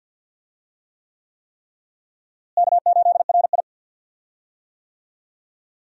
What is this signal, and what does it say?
Morse code, heard as a keyed tone, about 49 wpm, 700 Hz. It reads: K8RI
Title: USA Call Signs - 4 Characters - Repeated 3x - 50wpm